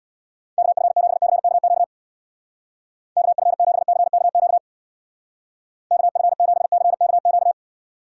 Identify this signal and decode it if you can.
Morse code, heard as a keyed tone, about 59 wpm, 700 Hz. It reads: KF7CKQ KF7CKQ KF7CKQ
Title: Speed Racing - USA 2x3 Call Signs 50wpm